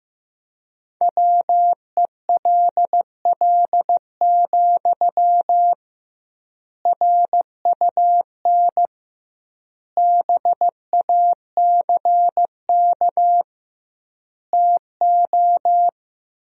Morse code, heard as a keyed tone, about 16 wpm, 700 Hz. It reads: WELL, RUN BACK TO
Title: 2x Word-Spacing - Sentences from Top 300 Words - 15wpm